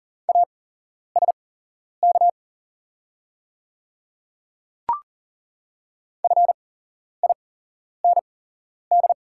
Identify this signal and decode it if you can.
Morse code, heard as a keyed tone, about 40 wpm, spaced as for 11 wpm, 690 Hz. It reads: ASK FIND
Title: 8x Character Spacing - Top 500 Words - 40wpm